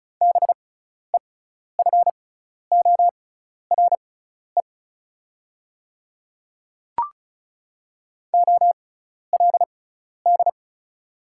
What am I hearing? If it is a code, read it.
Morse code, heard as a keyed tone, about 35 wpm, spaced as for 12 wpm, 700 Hz. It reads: BEFORE OLD
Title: 6x Character Spacing - Top 200 Words - 35wpm